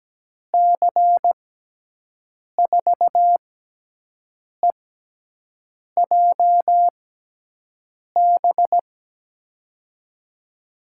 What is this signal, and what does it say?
Morse code, heard as a keyed tone, about 17 wpm, spaced as for 6 wpm, 700 Hz. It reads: C4EJB
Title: USA Call Signs - 6x Character Spacing - 17wpm